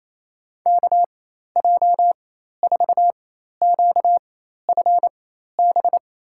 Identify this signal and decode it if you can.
Morse code, heard as a keyed tone, about 28 wpm, spaced as for 13 wpm, 700 Hz. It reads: KJ4QFB
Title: USA Call Signs - 6 Characters - 4x Character Spacing - 28wpm